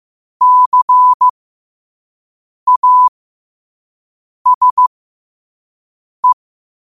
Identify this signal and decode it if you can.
Morse code, heard as a keyed tone, about 15 wpm, spaced as for 5 wpm, 1000 Hz. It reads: CASE
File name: qotdfiles / 5wpm